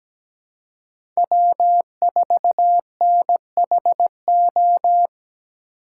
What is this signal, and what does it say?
Morse code, heard as a keyed tone, about 17 wpm, 700 Hz. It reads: W4NHO
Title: USA Call Signs - 5 Characters - 3x Character Spacing - 17wpm